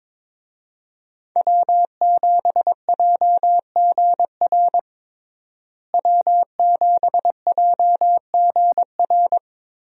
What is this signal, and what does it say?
Morse code, heard as a keyed tone, about 22 wpm, 700 Hz. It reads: W7JGR W7JGR
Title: USA Call Signs - 5 Characters - Repeated 3x - 22wpm